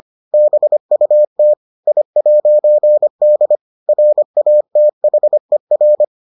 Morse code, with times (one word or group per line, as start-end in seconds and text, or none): 0.3-1.5 BUT
1.9-3.6 I'D
3.9-6.0 RATHER